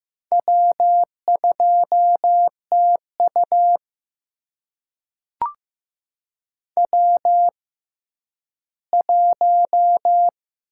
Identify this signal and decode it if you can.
Morse code, heard as a keyed tone, about 15 wpm, spaced as for 10 wpm, 700 Hz. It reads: W2TU W1
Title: USA Call Signs - 4 Characters - 6x Character Spacing - 15wpm